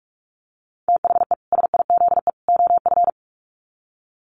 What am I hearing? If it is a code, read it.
Morse code, heard as a keyed tone, about 44 wpm, 700 Hz. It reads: THE SIZE OF
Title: Sets of 3 Words 45wpm